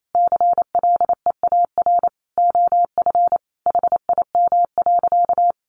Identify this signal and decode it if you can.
Morse code, heard as a keyed tone, about 28 wpm, 700 Hz. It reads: CLEAR OF HIM.
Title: Sentences from Top 500 Words 28wpm